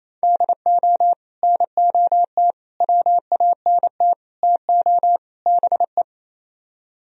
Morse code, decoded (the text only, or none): DO NOT WANT TO BE